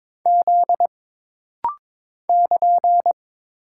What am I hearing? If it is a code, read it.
Morse code, heard as a keyed tone, about 22 wpm, 700 Hz. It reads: Z <KN>
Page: This rapid-fire timing is geared to push your ICR (Instant Character Recognition) skills.